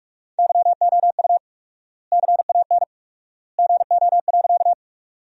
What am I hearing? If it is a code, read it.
Morse code, heard as a keyed tone, about 46 wpm, 700 Hz. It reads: YOU CAN GO.
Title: 4x Word-Spacing - Sentences from Top 100 Words - 45wpm